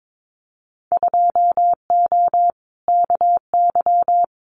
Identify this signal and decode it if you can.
Morse code, heard as a keyed tone, about 22 wpm, 700 Hz. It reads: 2O KY
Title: Winter Field Day Exchanges 22wpm